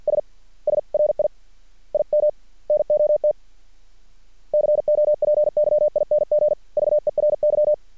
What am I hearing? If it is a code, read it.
Morse code, heard as a keyed tone, about 50 wpm, 600 Hz. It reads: S SRI IM NOT COPYING VERY
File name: archived / CRSnet